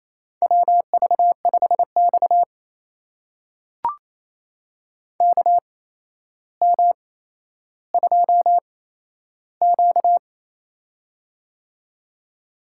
Morse code, heard as a keyed tone, about 28 wpm, spaced as for 12 wpm, 700 Hz. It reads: WV5X KM2Q
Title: USA Call Signs - 4 Characters - 8x Character Spacing - 28wpm